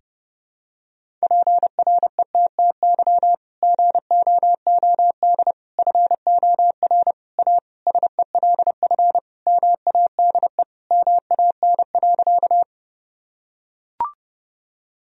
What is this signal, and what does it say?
Morse code, heard as a keyed tone, about 30 wpm, 700 Hz. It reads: PRETTY GOOD FOR A SELF MADE MAN.
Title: Sentences from Top 1000 Words - Review Part 1 - 30wpm